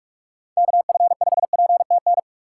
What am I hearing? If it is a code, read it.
Morse code, heard as a keyed tone, about 47 wpm, 700 Hz. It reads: KF5PTN
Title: USA Call Signs - 6 Characters - 6x Character Spacing - 45wpm